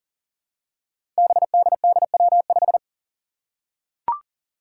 Morse code, heard as a keyed tone, about 40 wpm, 700 Hz. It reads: BDDW5
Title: Five Challenging Letter-Number Combinations 40wpm